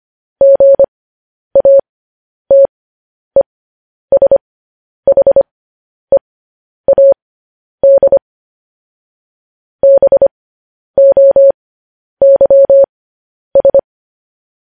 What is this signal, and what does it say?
Morse code, heard as a keyed tone, about 25 wpm, spaced as for 10 wpm, 550 Hz. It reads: GATESHEAD BOYS
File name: podcast / News headlines-1x-25x10